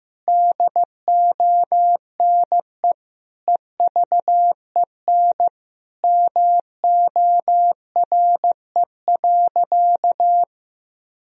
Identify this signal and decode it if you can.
Morse code, heard as a keyed tone, about 16 wpm, 700 Hz. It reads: DONE EVEN MORE.
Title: Sentences from Top 1000 Words - Review Part 1 - 15wpm